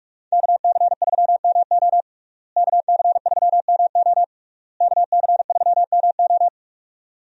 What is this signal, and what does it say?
Morse code, heard as a keyed tone, about 42 wpm, 700 Hz. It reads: KC3MO KC3MO KC3MO
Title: USA Call Signs - 5 Characters - Repeated 3x - 45wpm